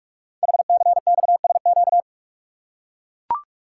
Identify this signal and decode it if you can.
Morse code, heard as a keyed tone, about 44 wpm, 700 Hz. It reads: HCXSQ
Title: Code Groups - 5 Letters - 45wpm